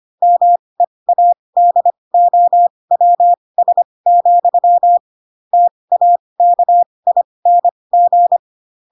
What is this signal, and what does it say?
Morse code, heard as a keyed tone, about 25 wpm, spaced as for 20 wpm, 700 Hz. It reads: MEADOWS, TAKING
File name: Pinocchio / Pinocchio ch4 15 25 ik1hge